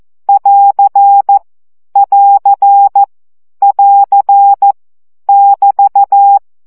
Morse code, heard as a keyed tone, about 14 wpm, 800 Hz. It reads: <AR> <AR> <AR> <BT>
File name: CW gyakorlas plusz 12 baud